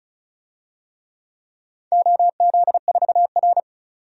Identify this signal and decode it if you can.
Morse code, heard as a keyed tone, about 36 wpm, 700 Hz. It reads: OZ4R
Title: DXCC-100 Call Signs - 4 Characters - 35wpm